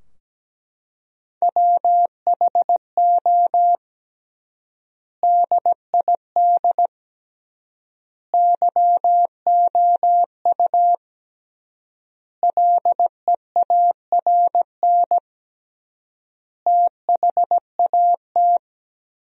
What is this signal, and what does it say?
Morse code, heard as a keyed tone, about 17 wpm, 700 Hz. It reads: WHO DID YOU LEARN THAT
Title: 3x Word-Spacing - Sentences from Top 300 Words - 17wpm